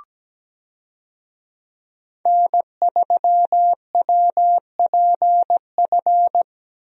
Morse code, heard as a keyed tone, about 17 wpm, 700 Hz. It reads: N3WPF
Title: USA Call Signs - 5 Characters - Repeated 3x - 17wpm